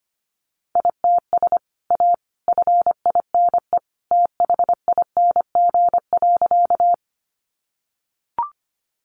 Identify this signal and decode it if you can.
Morse code, heard as a keyed tone, about 25 wpm, 700 Hz. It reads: ITS A FINE THING.
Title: Sentences from Top 600 Words 25wpm